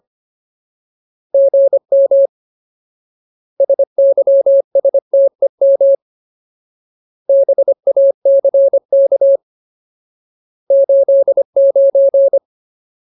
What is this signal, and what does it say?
Morse code, heard as a keyed tone, about 25 wpm, 550 Hz. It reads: GM SYSTEM BACK 89
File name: Ultimate Word List / 25WPM Ultimate